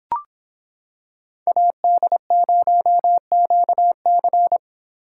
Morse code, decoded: AD0QC

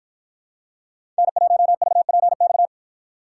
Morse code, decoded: N1VPX

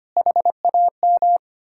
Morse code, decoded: HAM

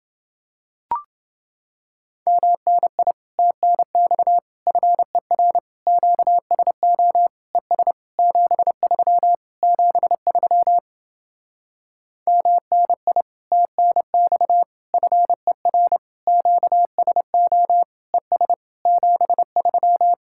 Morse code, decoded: MNI TNX FER QSO ES 73 73 MNI TNX FER QSO ES 73

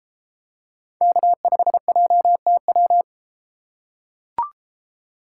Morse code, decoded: K5JTW

